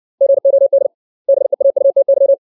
Morse code, heard as a keyed tone, about 64 wpm, 550 Hz. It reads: GOD BEAUTY
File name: 632 words / 632 Words 60WPM